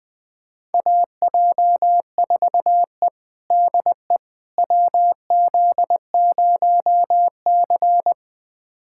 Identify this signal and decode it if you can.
Morse code, heard as a keyed tone, about 20 wpm, 700 Hz. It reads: AJ4E DE WZ0C
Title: USA Callsign DE Callsign - 4 Characters - 20wpm